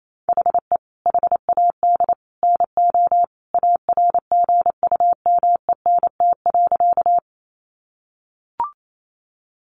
Morse code, heard as a keyed tone, about 28 wpm, 700 Hz. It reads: HE HAD NO ARGUMENT.
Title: Sentences from Top 1400 Words 28wpm